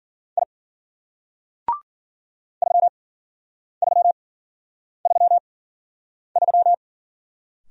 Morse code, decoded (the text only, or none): I 3333